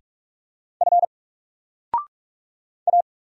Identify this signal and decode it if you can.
Morse code, heard as a keyed tone, about 46 wpm, 700 Hz. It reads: F A
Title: Single Letter-Number 45wpm